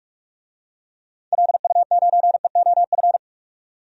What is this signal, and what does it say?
Morse code, heard as a keyed tone, about 42 wpm, 700 Hz. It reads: LU9EOF
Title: DXCC-100 Call Signs - 6 Characters - 45wpm